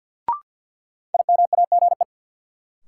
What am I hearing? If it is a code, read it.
Morse code, heard as a keyed tone, about 46 wpm, 700 Hz. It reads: IMAGE
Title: Sentences from Top 700 Words - New Word Drill - 50wpm